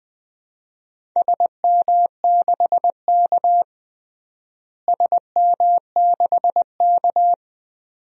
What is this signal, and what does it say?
Morse code, heard as a keyed tone, about 19 wpm, 700 Hz. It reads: SM6K SM6K
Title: DXCC-100 Call Signs - 4 Characters - Repeated 2x - 20wpm